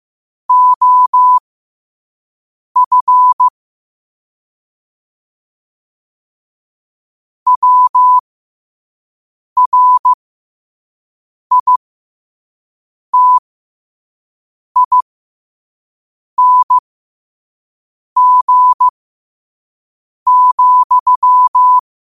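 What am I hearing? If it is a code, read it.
Morse code, heard as a keyed tone, about 15 wpm, spaced as for 5 wpm, 1000 Hz. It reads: OF WRITING,
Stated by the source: Morse code practice at 5 wpm.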